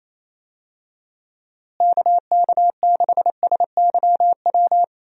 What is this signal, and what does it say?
Morse code, heard as a keyed tone, about 27 wpm, 700 Hz. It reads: KK6SYW
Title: USA Call Signs - 6 Characters - Repeated 3x - 28wpm